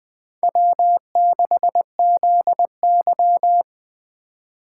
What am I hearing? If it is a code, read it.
Morse code, heard as a keyed tone, about 20 wpm, 700 Hz. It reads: W6ZY